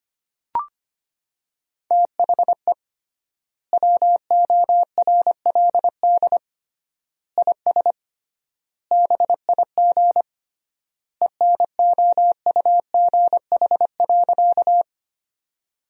Morse code, heard as a keyed tone, about 25 wpm, 700 Hz. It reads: THE WORLD IS BIG ENOUGH.
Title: 3x Word-Spacing - Sentences from Top 400 Words - 25wpm